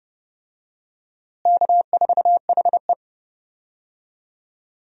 Morse code, heard as a keyed tone, about 30 wpm, 700 Hz. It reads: K4HE
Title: USA Call Signs - 4 Characters - Repeated 2x - 30wpm